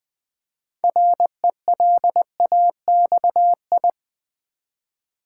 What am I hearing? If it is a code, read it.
Morse code, heard as a keyed tone, about 20 wpm, 700 Hz. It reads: RELAXI
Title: Build A Word - 2 Letters At A Time - 8-Letter Words - 20wpm